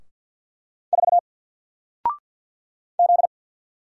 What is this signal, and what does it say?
Morse code, heard as a keyed tone, about 50 wpm, 700 Hz. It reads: 4 Z
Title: Single Letter-Number 50wpm